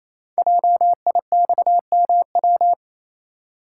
Morse code, heard as a keyed tone, about 27 wpm, 700 Hz. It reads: JIXMW
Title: Code Groups - 5 Letters - 28wpm